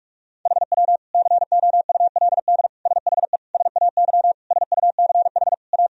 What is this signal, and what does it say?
Morse code, heard as a keyed tone, about 46 wpm, 700 Hz. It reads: HW COULD SHE SAY SUCH A
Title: Sentences from Top 200 Words and Abbreviations 45wpm